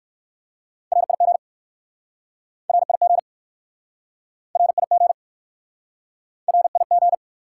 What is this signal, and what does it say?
Morse code, heard as a keyed tone, about 61 wpm, 700 Hz. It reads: RIG RIG RIG RIG